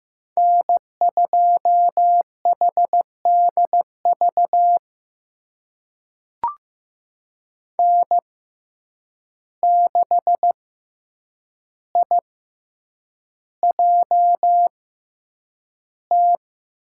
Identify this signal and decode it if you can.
Morse code, heard as a keyed tone, about 15 wpm, spaced as for 8 wpm, 700 Hz. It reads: N2HDV N6IJT
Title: USA Call Signs - 5 Characters - 6x Character Spacing - 15wpm